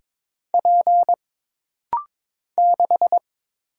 Morse code, heard as a keyed tone, about 22 wpm, 700 Hz. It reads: P 6
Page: This rapid-fire timing is geared to push your ICR (Instant Character Recognition) skills.